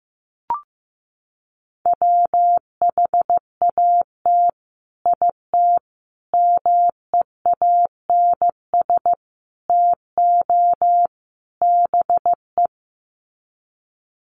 Morse code, heard as a keyed tone, about 15 wpm, 700 Hz. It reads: WHAT IT MEANS TO BE
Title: Sets of 5 Words 15wpm